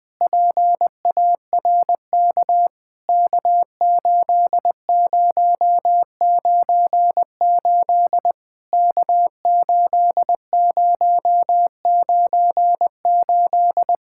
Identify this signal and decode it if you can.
Morse code, heard as a keyed tone, about 20 wpm, 700 Hz. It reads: PARK K8098 K8098
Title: POTA QSO Practice - 20wpm